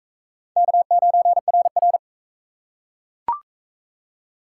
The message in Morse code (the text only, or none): K9RR